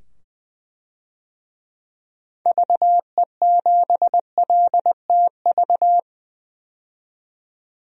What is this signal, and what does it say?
Morse code, heard as a keyed tone, about 20 wpm, 700 Hz. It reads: VE7LTV
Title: DXCC-100 Call Signs - 6 Characters - 20wpm